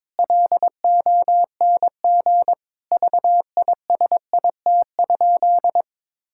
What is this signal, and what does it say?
Morse code, heard as a keyed tone, about 22 wpm, 700 Hz. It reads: LONG VISIT?
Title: Sentences from Top 1000 Words - Review Part 6 - 22wpm